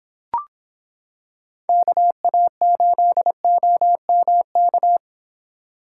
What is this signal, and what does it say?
Morse code, heard as a keyed tone, about 26 wpm, 700 Hz. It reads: KA8OMK